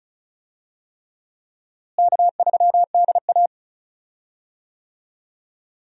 Morse code, heard as a keyed tone, about 36 wpm, 700 Hz. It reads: K3DA
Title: USA Call Signs - 4 Characters - Repeated 3x - 35wpm